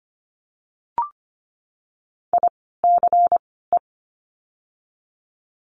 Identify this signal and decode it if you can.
Morse code, heard as a keyed tone, about 25 wpm, 700 Hz. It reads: I C E
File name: course / 25:10wpm / 094 Lesson 16f - Letter c - Words 25wpm